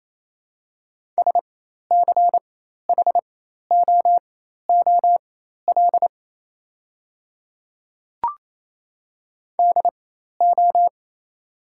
Morse code, heard as a keyed tone, about 28 wpm, spaced as for 13 wpm, 690 Hz. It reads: SCHOOL DO
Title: 4x Character Spacing - Top 300 Words - 28wpm